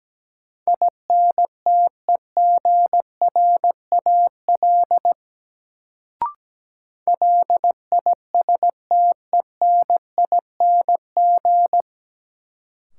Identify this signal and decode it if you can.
Morse code, heard as a keyed tone, about 17 wpm, 700 Hz. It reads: INTEGRAL LISTENING
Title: Sets of 5 Words - New Word Drill - 17wpm